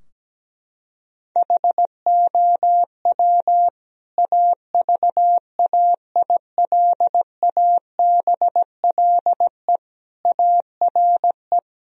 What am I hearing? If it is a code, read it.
Morse code, heard as a keyed tone, about 17 wpm, 700 Hz. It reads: HOW AVAILABLE ARE
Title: Sentences from Top 1000 Words - Review Part 3 - 17wpm